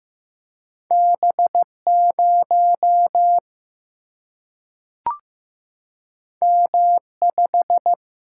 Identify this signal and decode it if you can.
Morse code, heard as a keyed tone, about 15 wpm, 700 Hz. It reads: B0 M5
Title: Single-Letter Single-Number 15wpm